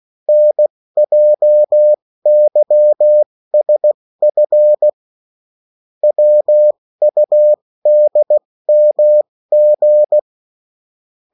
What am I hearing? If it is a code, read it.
Morse code, heard as a keyed tone, about 16 wpm, spaced as for 14 wpm, 600 Hz. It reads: NJYSF WUDMG
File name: morse mp3 / revision9-vitesse 14 mots